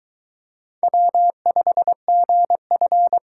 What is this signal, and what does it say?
Morse code, heard as a keyed tone, about 22 wpm, 700 Hz. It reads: W5GF